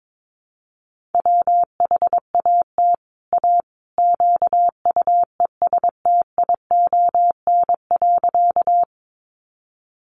Morse code, heard as a keyed tone, about 22 wpm, 700 Hz. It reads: WHAT A QUESTION.
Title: Sentences from Top 400 Words 22wpm